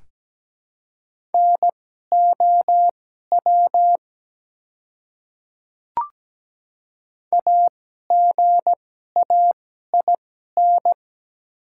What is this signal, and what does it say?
Morse code, heard as a keyed tone, about 17 wpm, spaced as for 12 wpm, 690 Hz. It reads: NOW AGAIN
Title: 2x Character Spacing - Top 300 Words - 17wpm